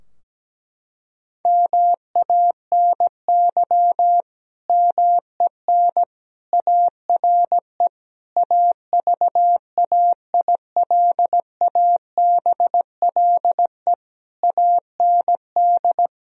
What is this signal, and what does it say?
Morse code, heard as a keyed tone, about 17 wpm, 700 Hz. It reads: MANY MEN ARE AVAILABLE AND
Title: Sentences from Top 1000 Words - Review Part 3 - 17wpm